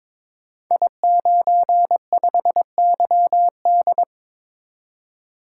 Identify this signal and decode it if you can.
Morse code, heard as a keyed tone, about 22 wpm, 700 Hz. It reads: I95YD
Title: Code Groups - 5 Letter:Numbers - 22wpm